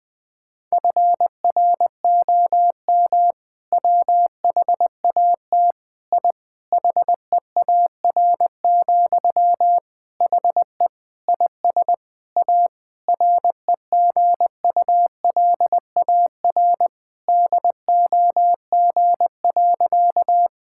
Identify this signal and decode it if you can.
Morse code, heard as a keyed tone, about 20 wpm, 700 Hz. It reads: FROM WHAT I HEAR, HE IS A REGULAR DOG.
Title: Sentences from Top 1000 Words - Review Part 6 - 20wpm